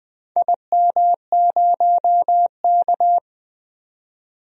Morse code, heard as a keyed tone, about 21 wpm, 700 Hz. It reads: IM0K
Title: DXCC-100 Call Signs - 4 Characters - 20wpm